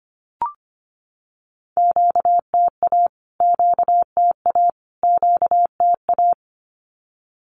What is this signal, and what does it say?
Morse code, heard as a keyed tone, about 25 wpm, 700 Hz. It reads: QTA QTA QTA
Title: Q-Codes Repeated 3x 25wpm